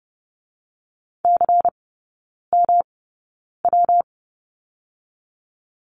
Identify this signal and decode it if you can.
Morse code, heard as a keyed tone, about 30 wpm, spaced as for 9 wpm, 700 Hz. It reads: CMW
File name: Lesson 19e - Letter w - Three Characters 30wpm